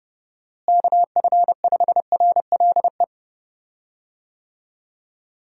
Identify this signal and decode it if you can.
Morse code, heard as a keyed tone, about 28 wpm, 700 Hz. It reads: KF5RLE
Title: USA Call Signs - 6 Characters - Repeated 3x - 30wpm